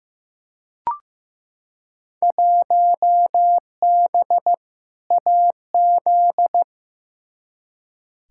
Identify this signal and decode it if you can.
Morse code, heard as a keyed tone, about 15 wpm, 700 Hz. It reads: 1B AZ
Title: Field Day Exchanges - 15wpm